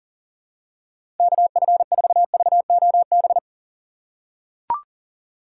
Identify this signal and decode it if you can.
Morse code, heard as a keyed tone, about 40 wpm, 700 Hz. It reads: KF4VOB
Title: USA Call Signs - 6 Characters - 2x Character Spacing - 40wpm